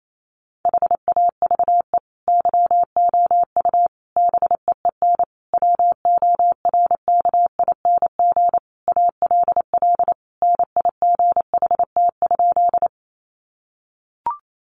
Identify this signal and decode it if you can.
Morse code, heard as a keyed tone, about 28 wpm, 700 Hz. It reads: HAVE YOU BEEN WORKING ALL NIGHT?
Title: Sentences from Top 700 Words 28wpm